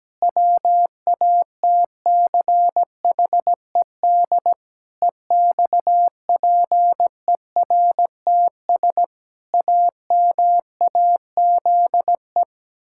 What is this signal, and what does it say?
Morse code, heard as a keyed tone, about 17 wpm, 700 Hz. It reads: WATCHED EXPERTS AMAZE
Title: Pangram Challenge v2 - 17wpm